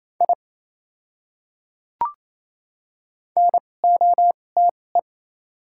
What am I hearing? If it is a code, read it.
Morse code, heard as a keyed tone, about 28 wpm, spaced as for 20 wpm, 700 Hz. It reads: I NOTE
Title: 2x Character Spacing - Top 500 Words - 28wpm